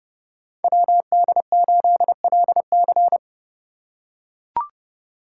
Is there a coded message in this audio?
Yes: Morse code from 0.6 to 3.2 s, heard as a keyed tone, about 30 wpm, 700 Hz. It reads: WD8LC